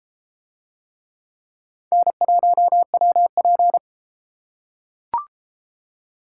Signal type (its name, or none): Morse code